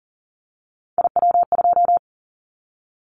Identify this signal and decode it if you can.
Morse code, heard as a keyed tone, about 40 wpm, 700 Hz. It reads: IW2
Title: Callsign Prefixes for Top 25 DXCC - 40wpm